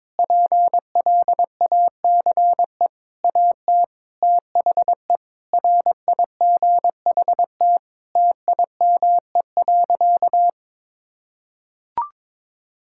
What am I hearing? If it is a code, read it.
Morse code, heard as a keyed tone, about 22 wpm, 700 Hz. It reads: PLACE AT THE RIGHT TIME.
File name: practice / 22wpm / Sentences with 8-Word Phrases - 22wpm